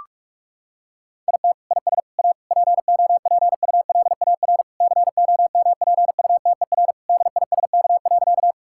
Morse code, heard as a keyed tone, about 46 wpm, 700 Hz. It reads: IT IS A POPULAR COMPUTER DISK.
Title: Sentences from Top 1000 Words - Review Part 6 - 45wpm